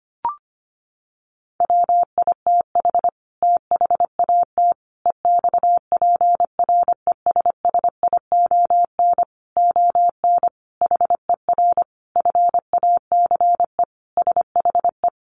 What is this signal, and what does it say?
Morse code, heard as a keyed tone, about 25 wpm, 700 Hz. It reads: WITH THAT EXPRESSION ON HER FACE SHE
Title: Sentences from Top 1200 Words 25wpm